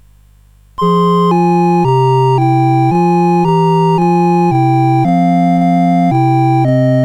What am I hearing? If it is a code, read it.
There is no code.